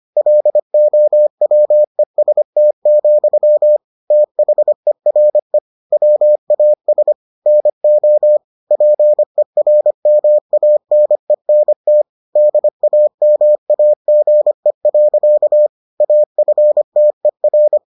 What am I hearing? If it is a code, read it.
Morse code, heard as a keyed tone, about 25 wpm, 600 Hz. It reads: LOWEST, THERE WAS NO PERMANENT DAMAGE. AFTER